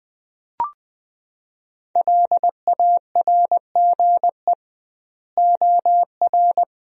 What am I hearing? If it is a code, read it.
Morse code, heard as a keyed tone, about 20 wpm, 700 Hz. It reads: LARGE OR
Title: 2x Word-Spacing - Sentences from Top 200 Words - 20wpm